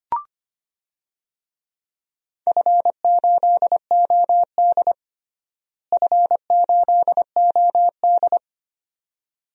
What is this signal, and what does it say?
Morse code, heard as a keyed tone, about 25 wpm, 700 Hz. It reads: F8OD F8OD
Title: DXCC-100 Call Signs - 4 Characters - Repeated 2x - 25wpm